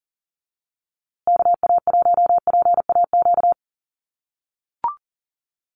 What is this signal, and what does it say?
Morse code, heard as a keyed tone, about 40 wpm, 700 Hz. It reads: KA1PAQ
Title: USA Call Signs Repeated 3x 40wpm